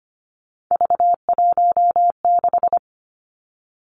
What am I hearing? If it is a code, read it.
Morse code, heard as a keyed tone, about 25 wpm, 700 Hz. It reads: V16
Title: Three Challenging Letter-Number Combinations 25wpm